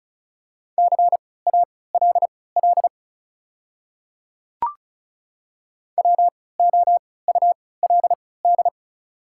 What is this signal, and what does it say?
Morse code, heard as a keyed tone, about 36 wpm, spaced as for 20 wpm, 690 Hz. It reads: CALL WOULD
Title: 3x Character Spacing - Top 100 Words - 35wpm